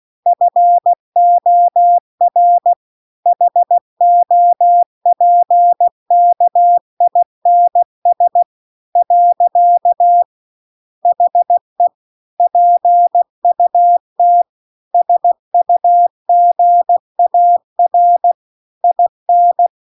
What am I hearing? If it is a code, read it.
Morse code, heard as a keyed tone, about 16 wpm, 700 Hz. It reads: FOR HOPKINS . HE PUT SUGAR IN